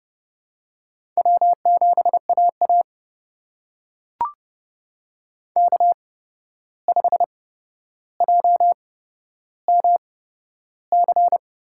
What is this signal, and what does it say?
Morse code, heard as a keyed tone, about 31 wpm, spaced as for 12 wpm, 700 Hz. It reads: W7AA K5JMC
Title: USA Call Signs - 8x Character Spacing - 30wpm